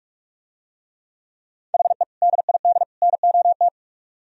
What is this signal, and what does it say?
Morse code, heard as a keyed tone, about 45 wpm, 700 Hz. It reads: HE DID NOT